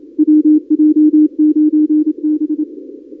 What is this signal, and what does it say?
Morse code, heard as a keyed tone, about 26 wpm, 300 Hz. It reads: WJ9B
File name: contests / 2019-cw-wpx-kt5j-k5pi- -2330 0-31